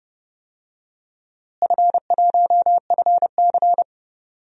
Morse code, heard as a keyed tone, about 29 wpm, 700 Hz. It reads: F1FC
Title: DXCC-100 Call Signs - 4 Characters - Repeated 2x - 30wpm